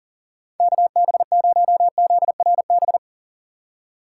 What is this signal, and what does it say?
Morse code, heard as a keyed tone, about 37 wpm, 700 Hz. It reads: KB0ZRB
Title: USA Call Signs - 6 Characters - 2x Character Spacing - 40wpm